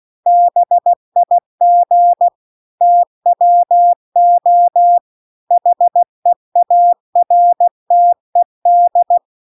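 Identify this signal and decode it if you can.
Morse code, heard as a keyed tone, about 16 wpm, 700 Hz. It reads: BIG TWO HEARTED
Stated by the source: Morse practice text at 16 WPM.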